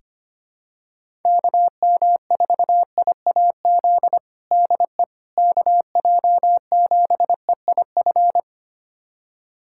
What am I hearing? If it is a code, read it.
Morse code, heard as a keyed tone, about 25 wpm, 700 Hz. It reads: KM4IAZ DE KJ7EIF
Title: USA Callsign DE Callsign 25wpm